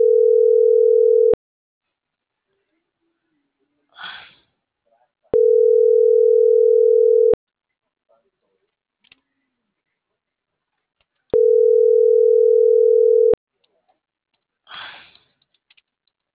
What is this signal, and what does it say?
Ringback tone (North America), 440 + 480 Hz.